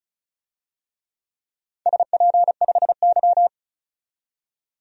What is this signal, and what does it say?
Morse code, heard as a keyed tone, about 36 wpm, 690 Hz. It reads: SP5Y
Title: DXCC-100 Call Signs - 4 Characters - 35wpm